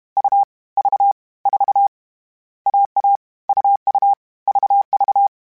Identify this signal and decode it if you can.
Morse code, heard as a keyed tone, about 32 wpm, 800 Hz. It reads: U V 4 AA UU VV